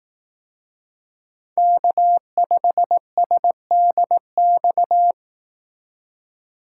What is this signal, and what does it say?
Morse code, heard as a keyed tone, about 18 wpm, 700 Hz. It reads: K5SDX
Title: Speed Racing - USA 1x3 Call Signs 15wpm